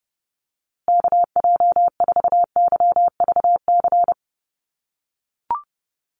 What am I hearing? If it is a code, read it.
Morse code, heard as a keyed tone, about 30 wpm, 700 Hz. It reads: KJ4YVC
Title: USA Call Signs Repeated 2x 30wpm